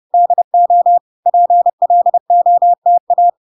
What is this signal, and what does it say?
Morse code, heard as a keyed tone, about 30 wpm, 700 Hz. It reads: DO PLOTA